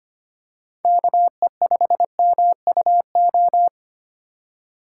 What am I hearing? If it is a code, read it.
Morse code, heard as a keyed tone, about 25 wpm, 700 Hz. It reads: KE5MUO
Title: USA Call Signs - 6x Character Spacing - 25wpm